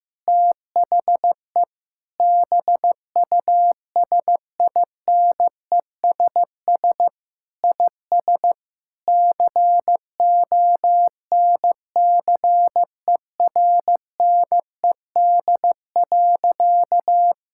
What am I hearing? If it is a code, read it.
Morse code, heard as a keyed tone, about 15 wpm, 700 Hz. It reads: THE BUSINESS IS CONCERNED.
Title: Sentences from Top 1000 Words - Review Part 6 - 15wpm